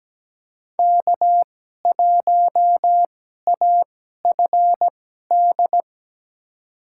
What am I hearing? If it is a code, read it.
Morse code, heard as a keyed tone, about 17 wpm, spaced as for 12 wpm, 700 Hz. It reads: K1AFD
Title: USA Call Signs - 5 Characters - 2x Character Spacing - 17wpm